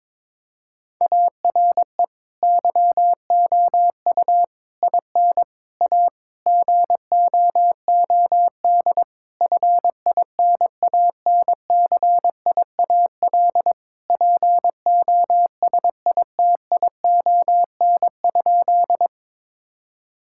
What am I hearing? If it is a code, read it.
Morse code, heard as a keyed tone, about 22 wpm, 700 Hz. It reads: ARE YOU IN A GOOD FINANCIAL POSITION?
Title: Sentences from Top 1000 Words - Review Part 5 - 22wpm